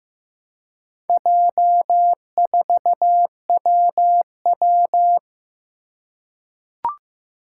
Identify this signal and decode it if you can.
Morse code, heard as a keyed tone, about 15 wpm, 700 Hz. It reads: J4WW
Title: Four Challenging Letter-Number Combinations 15wpm